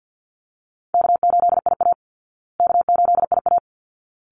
Morse code, heard as a keyed tone, about 50 wpm, 700 Hz. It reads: K8IA K8IA
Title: Call Signs Repeated 3x - 50wpm